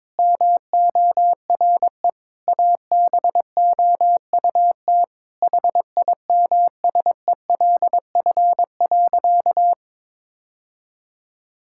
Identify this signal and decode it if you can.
Morse code, heard as a keyed tone, about 22 wpm, 700 Hz. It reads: MORE ABOUT HIMSELF.
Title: Sentences from Top 1600 Words 22wpm